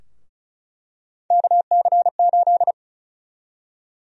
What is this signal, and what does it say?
Morse code, heard as a keyed tone, about 36 wpm, 700 Hz. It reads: KC8
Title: USA Call Sign Prefixes - Repeated 2x - 35wpm